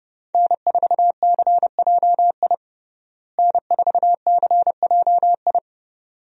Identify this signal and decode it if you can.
Morse code, heard as a keyed tone, about 30 wpm, 700 Hz. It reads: N4CJI N4CJI
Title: USA Call Signs - 5 Characters - Repeated 2x - 30wpm